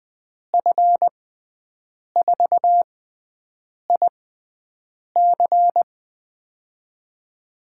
Morse code, heard as a keyed tone, about 20 wpm, spaced as for 7 wpm, 700 Hz. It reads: F4IC
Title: USA Call Signs - 5 Characters - 6x Character Spacing - 20wpm